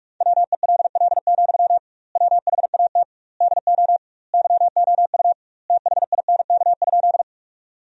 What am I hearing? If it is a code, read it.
Morse code, heard as a keyed tone, about 46 wpm, 700 Hz. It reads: WELL, WHAT DO YOU THINK?
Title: 2x Word-Spacing - Sentences from Top 200 Words - 45wpm